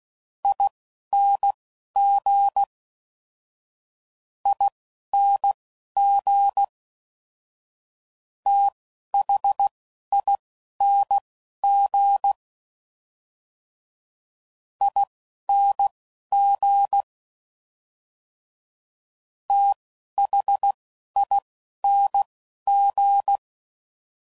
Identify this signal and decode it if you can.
Morse code, heard as a keyed tone, about 16 wpm, spaced as for 11 wpm, 800 Hz. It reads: ING ING THING ING THING